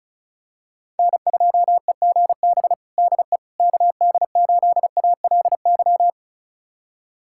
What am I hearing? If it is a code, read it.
Morse code, heard as a keyed tone, about 34 wpm, 690 Hz. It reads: N2EGB DE KD8ALY
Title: USA Callsign DE Callsign 35wpm